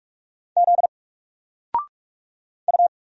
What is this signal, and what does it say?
Morse code, heard as a keyed tone, about 46 wpm, 700 Hz. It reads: Z U
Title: Single Character (Letter:Number:Punctuation:Prosign) 45wpm